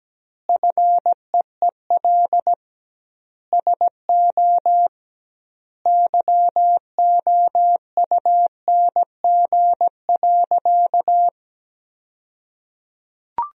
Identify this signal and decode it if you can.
Morse code, heard as a keyed tone, about 17 wpm, 700 Hz. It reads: FEEL SO YOUNG.
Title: 2x Word-Spacing - Sentences from Top 400 Words - 17wpm